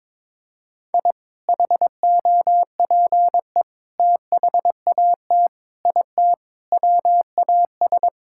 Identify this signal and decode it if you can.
Morse code, heard as a keyed tone, about 22 wpm, 700 Hz. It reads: I HOPE THAT IT WAS
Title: Sentences from Top 1000 Words - Review Part 5 - 22wpm